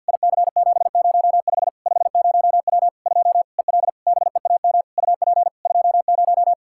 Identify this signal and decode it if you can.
Morse code, heard as a keyed tone, about 53 wpm, 700 Hz. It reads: IC705 50W 2 EL BEAM UP 20
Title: Practice QSO - Exchange 2 - Info - 50wpm